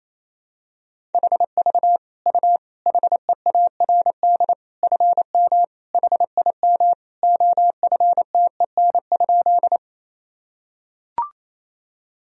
Morse code, heard as a keyed tone, about 27 wpm, 700 Hz. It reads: HV U HEARD FM HIM OFTEN?
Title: Sentences from Top 400 Words and Abbreviations 28wpm